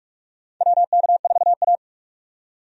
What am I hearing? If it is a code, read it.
Morse code, heard as a keyed tone, about 46 wpm, 700 Hz. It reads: WK4A
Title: USA Call Signs - 4 Characters - 45wpm